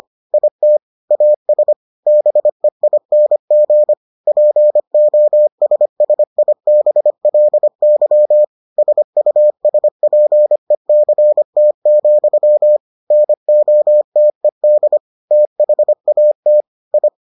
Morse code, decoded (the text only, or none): IT AS BEING POSSIBLY SUSPECT, NOTED THAT I